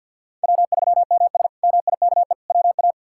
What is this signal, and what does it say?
Morse code, heard as a keyed tone, about 52 wpm, 700 Hz. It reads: W3MS MIKE WA
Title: K1USN SST QSO Practice - 50wpm